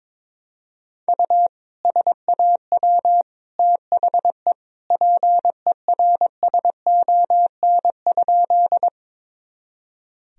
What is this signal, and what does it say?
Morse code, heard as a keyed tone, about 22 wpm, 700 Hz. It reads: U SAW THE PERSON?
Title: Sentences from Top 400 Words and Abbreviations 22wpm